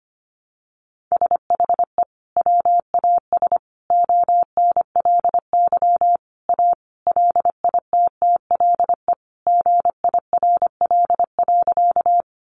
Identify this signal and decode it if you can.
Morse code, heard as a keyed tone, about 25 wpm, 700 Hz. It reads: SHE WAS ONLY A LITTLE GIRL.
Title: Sentences from Top 400 Words 25wpm